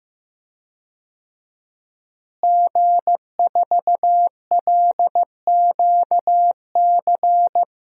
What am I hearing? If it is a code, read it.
Morse code, heard as a keyed tone, about 16 wpm, 700 Hz. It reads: G4LQC
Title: DXCC-100 Call Signs - 5 Characters - Repeated 2x - 15wpm